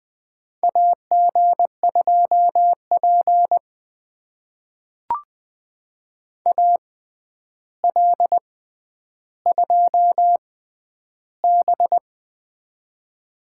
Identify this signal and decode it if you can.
Morse code, heard as a keyed tone, about 20 wpm, spaced as for 10 wpm, 700 Hz. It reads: AG2P AL2B
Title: USA Call Signs - 4 Characters - 6x Character Spacing - 20wpm